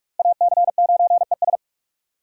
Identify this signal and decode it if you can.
Morse code, heard as a keyed tone, about 46 wpm, 700 Hz. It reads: AC9ES
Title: USA Call Signs - 3x Character Spacing - 45wpm